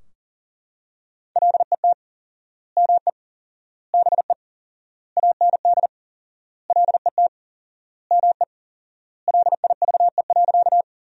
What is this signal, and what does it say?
Morse code, heard as a keyed tone, about 40 wpm, 700 Hz. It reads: LET ME BE AND LET ME LIVE.
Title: 4x Word-Spacing - Sentences from Top 300 Words - 40wpm